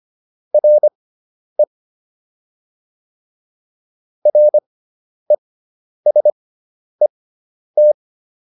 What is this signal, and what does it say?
Morse code, heard as a keyed tone, about 25 wpm, spaced as for 10 wpm, 600 Hz. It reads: RE RESET